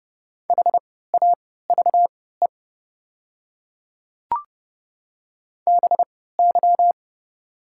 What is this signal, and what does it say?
Morse code, heard as a keyed tone, about 31 wpm, spaced as for 17 wpm, 700 Hz. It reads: HAVE BY
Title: 3x Character Spacing - Top 100 Words - 30wpm